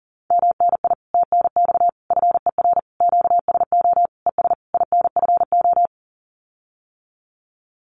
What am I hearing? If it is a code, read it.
Morse code, heard as a keyed tone, about 40 wpm, 700 Hz. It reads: MNI TNX FER QSO ES INFO
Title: Common QSO Phrases Encore 40wpm